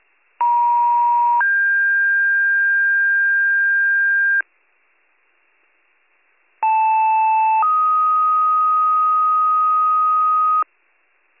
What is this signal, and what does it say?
Two-tone page, heard as a one-second tone, then a three-second tone, about 950 then 1600 Hz, about 880 then 1250 Hz. It is heard from 0.4 to 10.6 s.